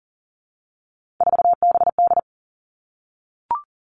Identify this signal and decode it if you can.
Morse code, heard as a keyed tone, about 40 wpm, 700 Hz. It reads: 4BD
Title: Three Challenging Letter-Number Combinations 40wpm